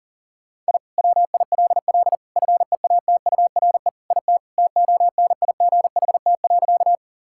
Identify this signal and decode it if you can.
Morse code, heard as a keyed tone, about 40 wpm, 700 Hz. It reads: I WILL FEATURE IT TONIGHT.
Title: Sentences from Top 1000 Words - Review Part 5 - 40wpm